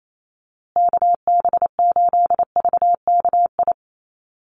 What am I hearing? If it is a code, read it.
Morse code, heard as a keyed tone, about 28 wpm, 700 Hz. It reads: KB8VKI
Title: USA Call Signs Repeated 3x 28wpm